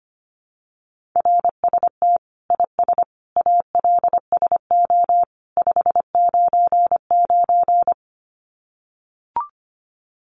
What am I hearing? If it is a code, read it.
Morse code, heard as a keyed tone, about 25 wpm, 700 Hz. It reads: RST IS ALSO 599
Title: Common QSO Phrases 25wpm